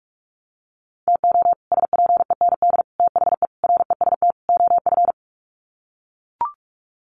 Morse code, heard as a keyed tone, about 44 wpm, 700 Hz. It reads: TO SPEND THE REST OF
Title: Sets of 5 Words Encore2 45wpm